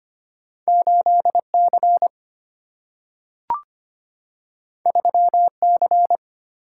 Morse code, heard as a keyed tone, about 25 wpm, 700 Hz. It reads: 8 C 3 C
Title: Single-Letter Single-Number 25wpm